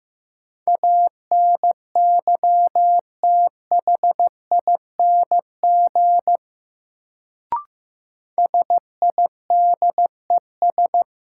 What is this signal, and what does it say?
Morse code, heard as a keyed tone, about 15 wpm, 700 Hz. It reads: ANYTHING SIDES
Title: Sets of 4 Words - New Word Drill - 15wpm